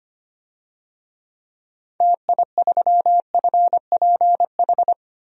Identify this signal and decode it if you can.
Morse code, heard as a keyed tone, about 25 wpm, 700 Hz. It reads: TI3FPH
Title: DXCC-100 Call Signs - 6 Characters - 25wpm